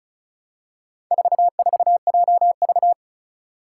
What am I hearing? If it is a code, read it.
Morse code, heard as a keyed tone, about 34 wpm, 700 Hz. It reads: 44JV
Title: Four Challenging Letter-Number Combinations 35wpm